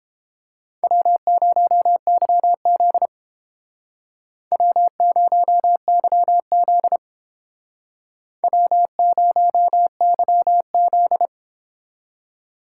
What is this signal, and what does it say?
Morse code, heard as a keyed tone, about 32 wpm, 700 Hz. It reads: W0YZ W0YZ W0YZ